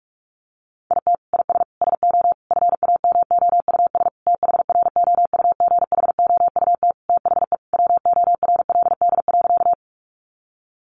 Morse code, heard as a keyed tone, about 44 wpm, 700 Hz. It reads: IT IS SO FAMOUS THROUGHOUT THE WORLD.
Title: Sentences from Top 1500 Words 45wpm